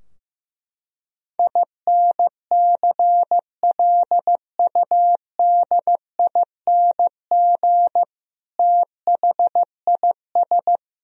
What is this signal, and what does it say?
Morse code, heard as a keyed tone, about 15 wpm, 700 Hz. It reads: INCLUDING THIS
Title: Sentences from Top 1000 Words - Review Part 4 - 15wpm